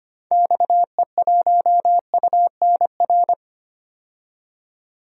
Morse code, heard as a keyed tone, about 25 wpm, 700 Hz. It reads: XE1UNR
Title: DXCC-100 Call Signs - 6 Characters - 25wpm